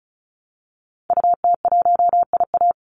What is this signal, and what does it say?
Morse code, heard as a keyed tone, about 34 wpm, 700 Hz. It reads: UT1IA
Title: Call Signs - 35wpm